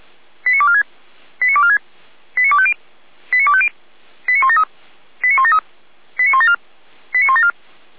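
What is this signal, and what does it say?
Five-tone selective call (ZVEI), heard as quick runs of five tones, tones of about 70 ms. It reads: 89246 89246 89250 89250 89173 89173 89174 89174
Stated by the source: ZVEI five-tone selective call received on an SDR, served by feuerwehr-orlamuende (Orlamünde fire brigade).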